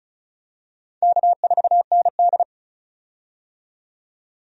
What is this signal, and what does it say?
Morse code, heard as a keyed tone, about 36 wpm, 700 Hz. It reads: K4ND
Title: USA Call Signs - 4 Characters - 35wpm